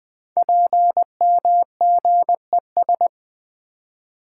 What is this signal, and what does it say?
Morse code, heard as a keyed tone, about 20 wpm, 700 Hz. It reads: PMGES
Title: Code Groups - 5 Letters - 20wpm